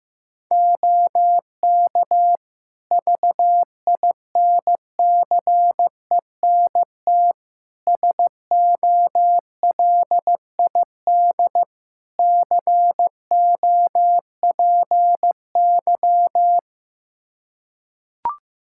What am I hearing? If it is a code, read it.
Morse code, heard as a keyed tone, about 16 wpm, 700 Hz. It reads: OK VINCENT SOLID COPY
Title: Practice QSO - Exchange 3 - More Info - 15wpm